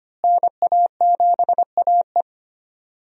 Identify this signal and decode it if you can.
Morse code, heard as a keyed tone, about 25 wpm, 700 Hz. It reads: NA7AE
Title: USA Call Signs - 5 Characters - Repeated 2x - 25wpm